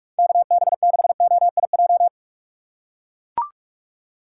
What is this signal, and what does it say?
Morse code, heard as a keyed tone, about 42 wpm, 700 Hz. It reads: KB6OIJ
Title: USA Call Signs - 6 Characters - 3x Character Spacing - 45wpm